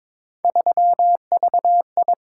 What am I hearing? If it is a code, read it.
Morse code, heard as a keyed tone, about 22 wpm, 700 Hz. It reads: 3VI
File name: practice / 22wpm / Three Letter-Number Combinations - 22wpm